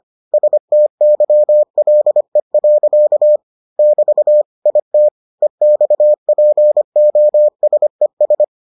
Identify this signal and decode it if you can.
Morse code, heard as a keyed tone, about 25 wpm, 600 Hz. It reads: STYLE. <BT> IT EXPOSES